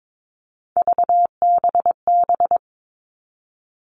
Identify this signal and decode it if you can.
Morse code, heard as a keyed tone, about 22 wpm, 700 Hz. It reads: VBB
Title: Three Challenging Letter-Number Combinations 22wpm